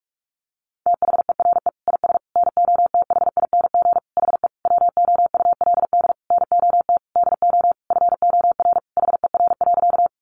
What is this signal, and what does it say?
Morse code, heard as a keyed tone, about 44 wpm, 700 Hz. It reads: THERE IS NOTHING HE WOULD NOT DO FOR HER.
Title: Sentences from Top 600 Words 45wpm